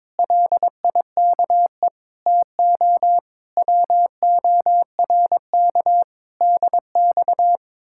Morse code, heard as a keyed tone, about 23 wpm, 700 Hz. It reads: LIKE TO WORK DX
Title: Speed-Racing Common QSO Phrases - 22wpm